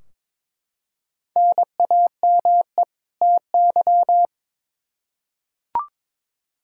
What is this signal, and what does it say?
Morse code, heard as a keyed tone, about 22 wpm, 700 Hz. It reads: NAME TY
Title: Practice QSO - Exchange 1 - Beginning - 22wpm